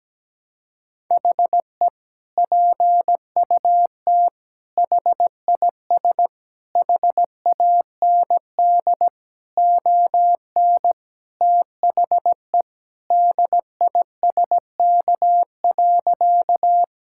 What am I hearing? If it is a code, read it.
Morse code, heard as a keyed tone, about 17 wpm, 700 Hz. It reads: HE PUT HIS HAND ON THE DISK.
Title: Sentences from Top 1000 Words - Review Part 6 - 17wpm